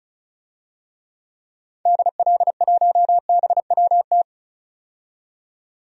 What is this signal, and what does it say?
Morse code, heard as a keyed tone, about 35 wpm, 700 Hz. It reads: DL1BWT
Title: DXCC-100 Call Signs - 6 Characters - 35wpm